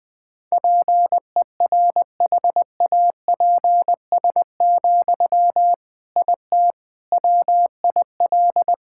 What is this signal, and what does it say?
Morse code, heard as a keyed tone, about 20 wpm, 700 Hz. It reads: PERHAPS, IT WIL
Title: Sentences from Top 1000 Words - Review Part 6 - 20wpm